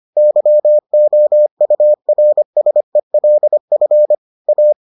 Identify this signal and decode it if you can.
Morse code, heard as a keyed tone, about 25 wpm, 600 Hz. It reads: YOURSELF A